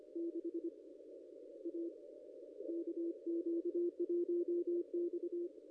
Morse code, heard as a keyed tone, about 25 wpm, 350 Hz. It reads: 6 A KQ1X
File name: contests / ss-cw-k5tr-op-n5rz -0200